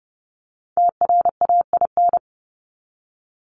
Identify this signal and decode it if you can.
Morse code, heard as a keyed tone, about 30 wpm, 700 Hz. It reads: TRAIN
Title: ING Suffix - Top 100 Words with Base Words 30wpm